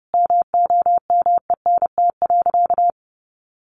Morse code, heard as a keyed tone, about 30 wpm, 700 Hz. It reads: MOMENT.
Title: Sentences from Top 1300 Words 30wpm